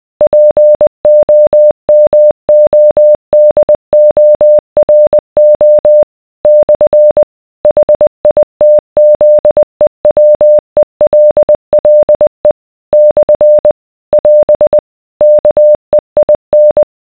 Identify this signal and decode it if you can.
Morse code, heard as a keyed tone, about 20 wpm, 600 Hz. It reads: POMODORO / HITZEWELLE / <AS> KEIN